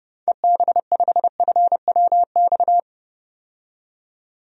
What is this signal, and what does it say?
Morse code, heard as a keyed tone, about 30 wpm, 700 Hz. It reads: EB5FWX
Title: DXCC-100 Call Signs - 6 Characters - 30wpm